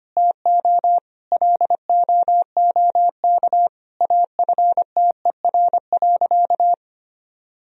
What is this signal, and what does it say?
Morse code, heard as a keyed tone, about 25 wpm, 700 Hz. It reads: TO LOOK AFTER.
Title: Sentences from Top 1000 Words - Review Part 3 - 25wpm